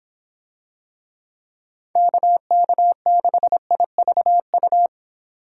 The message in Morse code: KK6IVU

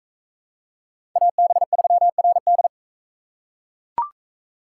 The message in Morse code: AB3RD

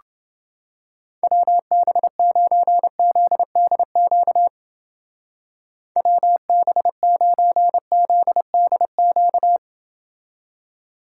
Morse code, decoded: WB9ZDQ WB9ZDQ

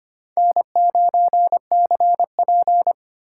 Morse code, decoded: N9CP

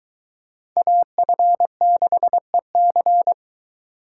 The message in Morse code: AF6EC

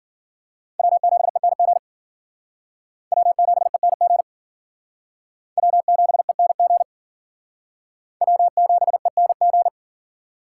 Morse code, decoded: W7ENG W7ENG W7ENG W7ENG

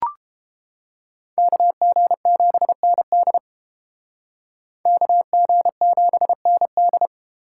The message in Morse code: KG7ND KG7ND

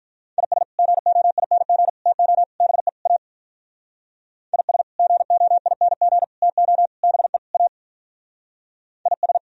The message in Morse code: IS GOING TO BE A IS GOING TO BE A IS

